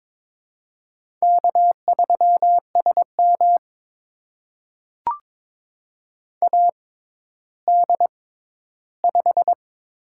Morse code, heard as a keyed tone, about 22 wpm, spaced as for 13 wpm, 700 Hz. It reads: K3SM AD5
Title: USA Call Signs - 4 Characters - 6x Character Spacing - 22wpm